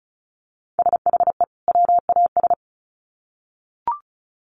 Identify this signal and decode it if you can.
Morse code, heard as a keyed tone, about 34 wpm, 700 Hz. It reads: SHE WAS
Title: Sets of 2 Words 35wpm